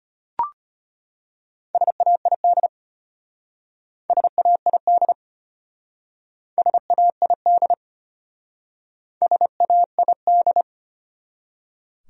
Morse code, heard as a keyed tone, about 32 wpm, 700 Hz. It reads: SAID SAID SAID SAID